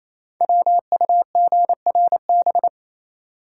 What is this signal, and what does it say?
Morse code, heard as a keyed tone, about 27 wpm, 700 Hz. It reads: WUGRB
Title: Code Groups - 5 Letters - 28wpm